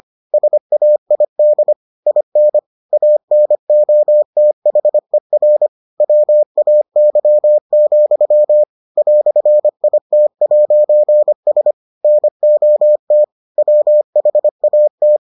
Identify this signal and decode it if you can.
Morse code, heard as a keyed tone, about 25 wpm, 600 Hz. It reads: SAID IN ANOTHER WAY, "IT'S NOT WHAT